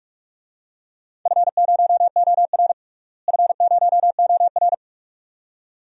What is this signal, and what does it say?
Morse code, heard as a keyed tone, about 44 wpm, 700 Hz. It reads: F0OR F0OR
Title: DXCC-100 Call Signs - 4 Characters - Repeated 2x - 45wpm